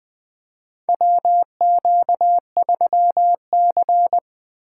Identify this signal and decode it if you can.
Morse code, heard as a keyed tone, about 20 wpm, 700 Hz. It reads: WQ3C
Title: USA Call Signs - 2x Character Spacing - 20wpm